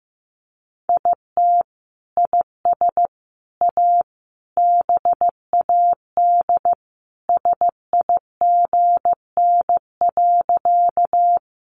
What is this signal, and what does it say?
Morse code, heard as a keyed tone, about 15 wpm, 700 Hz. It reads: IT IS A BAD SIGN.
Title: Sentences from Top 1000 Words 15wpm